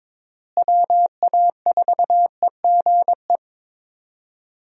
Morse code, heard as a keyed tone, about 22 wpm, 700 Hz. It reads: WA4EGE
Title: USA Call Signs - 6 Characters - Repeated 3x - 22wpm